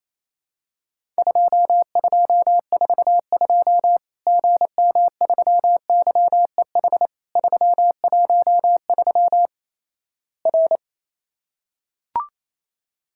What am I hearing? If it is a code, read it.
Morse code, heard as a keyed tone, about 28 wpm, 700 Hz. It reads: 2242 GM3YEH 313 R
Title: Worked All Europe DX Contest - QTC Exchange - 28wpm